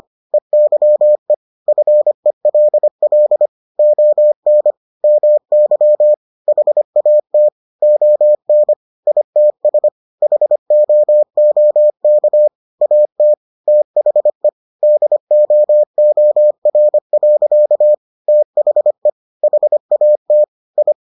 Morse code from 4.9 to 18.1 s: MY HAT ON ITS HOOK AT THE DOOR.

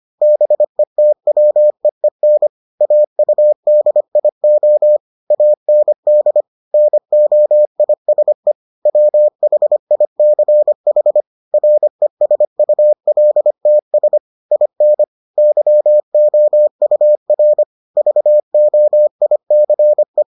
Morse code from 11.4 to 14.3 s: RESULTS

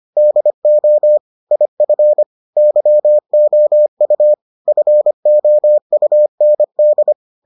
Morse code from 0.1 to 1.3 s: DO

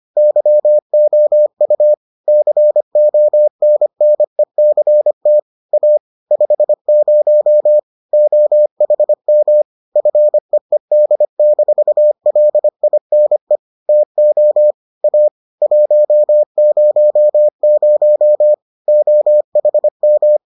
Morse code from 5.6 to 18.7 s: A 50 OHM FEED-LINE TO A 100

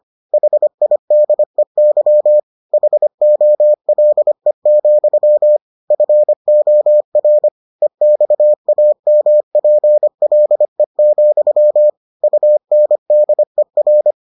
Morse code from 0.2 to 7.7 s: HIDEY HOLE, FOR

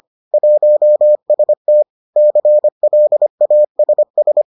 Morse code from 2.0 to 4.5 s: CLASS

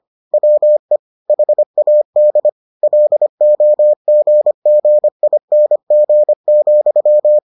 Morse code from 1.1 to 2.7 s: HAD

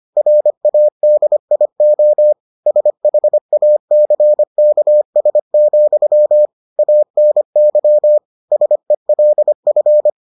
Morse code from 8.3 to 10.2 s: SELF

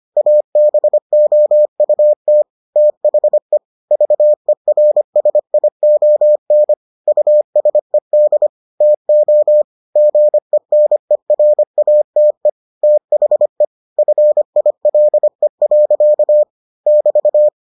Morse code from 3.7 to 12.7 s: VERSION USED TO GENERATE

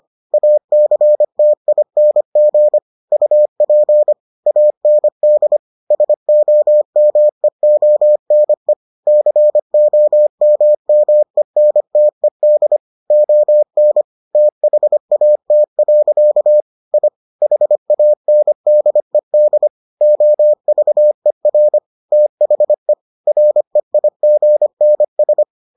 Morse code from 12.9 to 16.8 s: ON THAT.

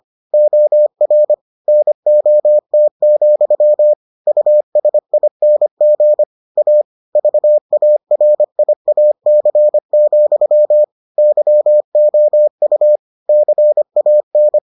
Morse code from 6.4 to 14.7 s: A VARIAC, YOU CAN

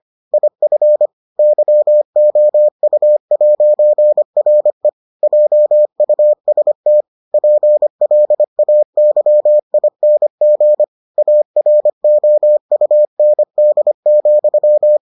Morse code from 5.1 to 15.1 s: JUST PLAYING AROUND,

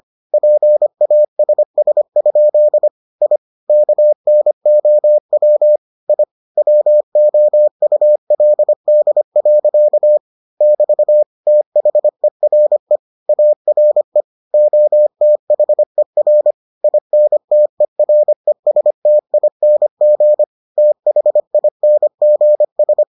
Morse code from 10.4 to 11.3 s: <BT>

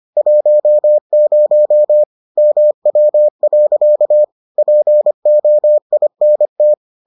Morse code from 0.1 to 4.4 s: 10 MW.

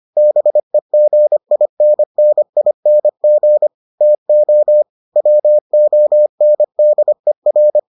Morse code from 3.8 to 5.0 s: TO

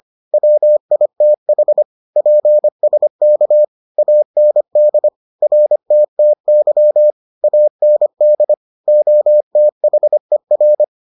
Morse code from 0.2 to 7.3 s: WITH PSK AND RTTY

